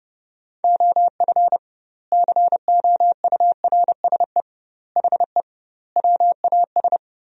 Morse code, heard as a keyed tone, about 32 wpm, 700 Hz. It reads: OF COURSE HE WAS
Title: 2x Word-Spacing - Sentences from Top 500 Words - 30wpm